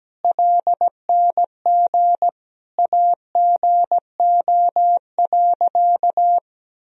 Morse code, heard as a keyed tone, about 17 wpm, 700 Hz. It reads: LNG AGO.
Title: Sentences from Top 500 Words and Abbreviations 17wpm